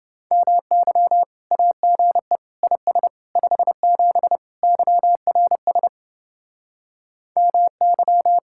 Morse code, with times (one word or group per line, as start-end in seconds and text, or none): 0.3-1.2 MY
1.5-2.4 AGE
2.6-3.1 IS
3.4-4.4 57
4.6-5.9 YRS
7.4-8.4 MY